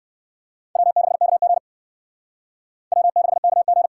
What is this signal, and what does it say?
Morse code, heard as a keyed tone, about 69 wpm, 700 Hz. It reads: W6KG W6KG